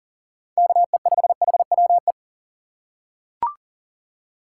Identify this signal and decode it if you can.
Morse code, heard as a keyed tone, about 40 wpm, 700 Hz. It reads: KE5HWE
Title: USA Call Signs - 6 Characters - 6x Character Spacing - 40wpm